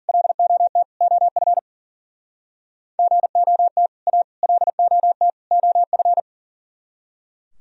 Morse code, heard as a keyed tone, about 50 wpm, 700 Hz. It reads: LOT OF GOT A LOT OF
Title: Speed-Racing - Sets of 4 Words 40wpm